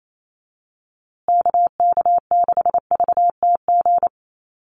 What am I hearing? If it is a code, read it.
Morse code, heard as a keyed tone, about 28 wpm, 700 Hz. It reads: KK6VTG
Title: USA Call Signs Repeated 2x 28wpm